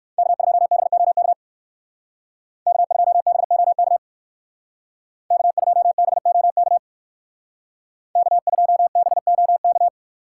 Morse code, heard as a keyed tone, about 54 wpm, 700 Hz. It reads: K2BOK K2BOK K2BOK K2BOK
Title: Speed Racing - USA Call Signs 45wpm